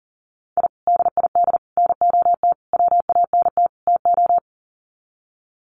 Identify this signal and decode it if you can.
Morse code, heard as a keyed tone, about 40 wpm, 700 Hz. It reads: I DID NOT WANT TO
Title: Sets of 5 Words Encore 40wpm